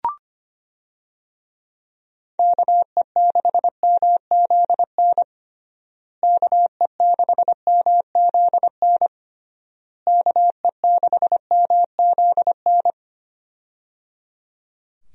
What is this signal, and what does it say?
Morse code, heard as a keyed tone, about 25 wpm, 700 Hz. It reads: KE6MZN KE6MZN KE6MZN
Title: USA Call Signs - 6 Characters - Repeated 3x - 25wpm